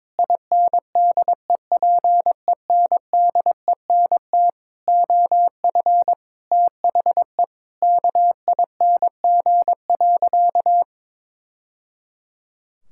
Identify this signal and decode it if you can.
Morse code, heard as a keyed tone, about 22 wpm, 700 Hz. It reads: INDEPENDENT OF THE KING.
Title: Sentences from Top 1000 Words - Review Part 6 - 22wpm